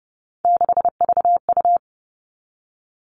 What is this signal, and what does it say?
Morse code, heard as a keyed tone, about 30 wpm, 700 Hz. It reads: 6VU
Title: All Cut Numbers - 3-Digit 30wpm